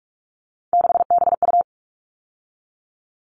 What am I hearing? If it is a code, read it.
Morse code, heard as a keyed tone, about 44 wpm, 700 Hz. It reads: 6BU
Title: All Cut Numbers - 3-Digit 45wpm